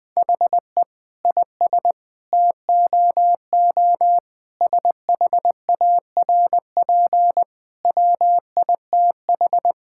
Morse code, heard as a keyed tone, about 20 wpm, 700 Hz. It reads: HE IS TOO SHARP WITH
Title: Sentences from Top 1600 Words 20wpm